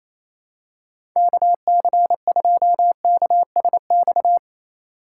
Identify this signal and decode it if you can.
Morse code, heard as a keyed tone, about 28 wpm, 700 Hz. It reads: KC2KSX
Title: USA Call Signs - 6 Characters - 4x Character Spacing - 28wpm